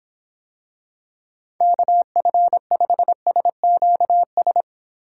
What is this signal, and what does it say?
Morse code, heard as a keyed tone, about 24 wpm, 700 Hz. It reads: KF5SQS